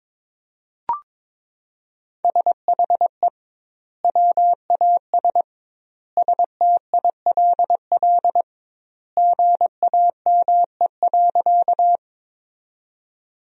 Morse code, heard as a keyed tone, about 21 wpm, 700 Hz. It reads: SHE WAS STILL GAME.
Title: 2x Word-Spacing - Sentences from Top 500 Words - 22wpm